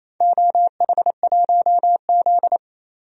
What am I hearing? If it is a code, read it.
Morse code, heard as a keyed tone, about 28 wpm, 700 Hz. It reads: OH1Z
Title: DXCC-100 Call Signs - 4 Characters - 28wpm